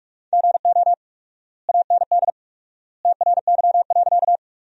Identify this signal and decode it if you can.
Morse code, heard as a keyed tone, about 42 wpm, 700 Hz. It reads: GO AND TRY.
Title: 4x Word-Spacing - Sentences from Top 200 Words - 45wpm